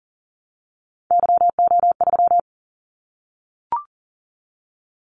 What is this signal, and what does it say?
Morse code, heard as a keyed tone, about 38 wpm, 700 Hz. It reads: YO3
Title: Callsign Prefixes for Top 25 DXCC - 40wpm